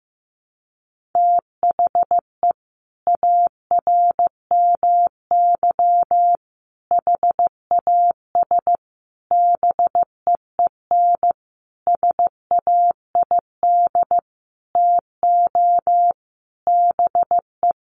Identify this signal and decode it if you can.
Morse code, heard as a keyed tone, about 15 wpm, 700 Hz. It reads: THE ARMY HAS BEEN SAID TO BE
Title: Sentences from Top 1400 Words 15wpm